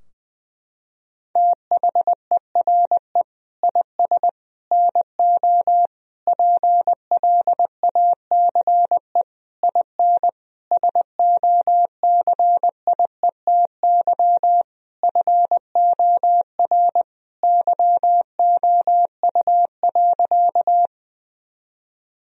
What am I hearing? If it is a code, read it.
Morse code, heard as a keyed tone, about 20 wpm, 700 Hz. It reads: THERE IS NO PLACE IN SOCIETY FOR YOU.
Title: Sentences from Top 1000 Words - Review Part 4 - 20wpm